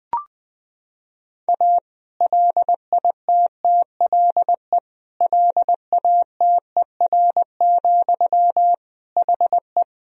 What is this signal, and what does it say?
Morse code, heard as a keyed tone, about 19 wpm, 700 Hz. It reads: A LITTLE LATER, HE
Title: Sentences from Top 1000 Words - Review Part 4 - 20wpm